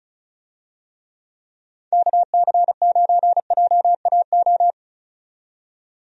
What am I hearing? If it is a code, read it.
Morse code, heard as a keyed tone, about 34 wpm, 700 Hz. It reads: KC9JAO
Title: USA Call Signs - 6 Characters - 35wpm